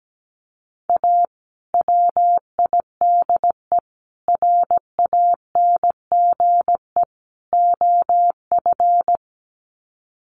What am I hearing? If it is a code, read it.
Morse code, heard as a keyed tone, about 17 wpm, 700 Hz. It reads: A WIDE RANGE OF
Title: Sets of 4 Words 17wpm